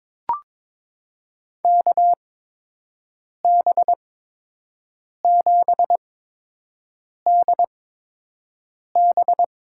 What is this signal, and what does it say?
Morse code, heard as a keyed tone, about 22 wpm, spaced as for 6 wpm, 700 Hz. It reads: KB7DB
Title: USA Call Signs - 6 Characters - 8x Character Spacing - 22wpm